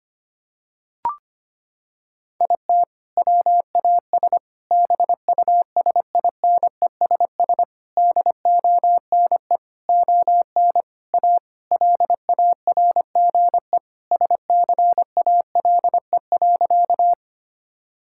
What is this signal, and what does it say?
Morse code, heard as a keyed tone, about 25 wpm, 700 Hz. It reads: IT WAS BUSINESS DONE ON A LARGE SCALE.
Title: Sentences from Top 1000 Words - Review Part 5 - 25wpm